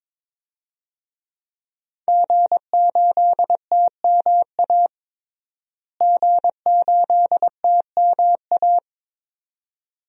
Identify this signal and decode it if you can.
Morse code, heard as a keyed tone, about 22 wpm, 700 Hz. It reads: G8TMA G8TMA
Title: DXCC-100 Call Signs - 5 Characters - Repeated 2x - 22wpm